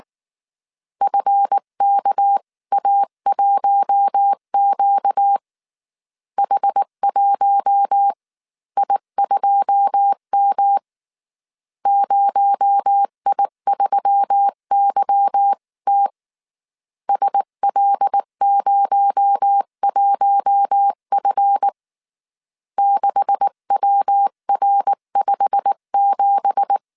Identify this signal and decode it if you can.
Morse code, heard as a keyed tone, about 19 wpm, 800 Hz. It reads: FKA1Q H1I2M 0I3YT SL01F 6WR57
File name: CW / DET / 1 9 1500C S B 18wpm 2551